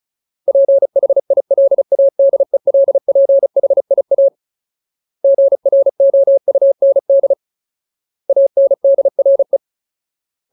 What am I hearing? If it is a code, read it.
Morse code, heard as a keyed tone, about 34 wpm, 550 Hz. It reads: PHILADELPHIA GROUND ANDRE
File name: Ultimate Word List / 35WPM Ultimate